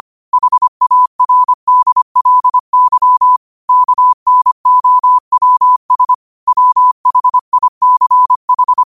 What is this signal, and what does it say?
Morse code, heard as a keyed tone, about 25 wpm, 1000 Hz. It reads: HARDLY KNOWS WHICH
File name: qotdfiles / 25wpm